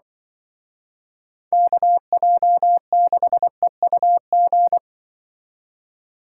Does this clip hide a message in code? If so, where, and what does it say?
Yes: Morse code from 1.5 to 4.8 s, heard as a keyed tone, about 25 wpm, 700 Hz. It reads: KJ6EUG